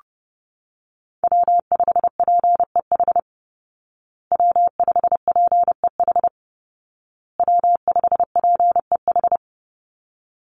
Morse code, heard as a keyed tone, about 30 wpm, 700 Hz. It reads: W5PEH W5PEH W5PEH
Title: Call Signs Repeated 2x - 30wpm